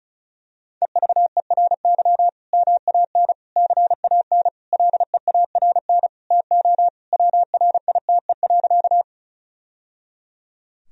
Morse code, heard as a keyed tone, about 36 wpm, 700 Hz. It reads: EVERY MAN CAN LEARN TO WRITE.
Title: Sentences from Top 1000 Words - Review Part 1 - 35wpm